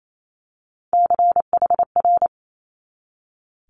Morse code, heard as a keyed tone, about 28 wpm, 700 Hz. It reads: CHR
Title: Multi-letter Phonemes 28wpm